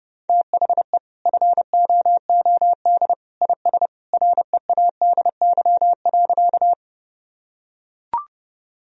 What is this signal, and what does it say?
Morse code, heard as a keyed tone, about 30 wpm, 700 Hz. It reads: THE FOOD IS READY.
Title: Sentences from Top 1000 Words - Review Part 1 - 30wpm